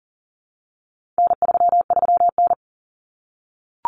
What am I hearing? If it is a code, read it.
Morse code, heard as a keyed tone, about 40 wpm, 700 Hz. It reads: N33N
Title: Common Cut Numbers - 4-Digit 40wpm